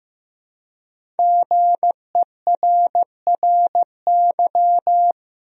Morse code, heard as a keyed tone, about 15 wpm, 700 Hz. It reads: GERRY
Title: USA Common Names - Max 5 Letters - 15wpm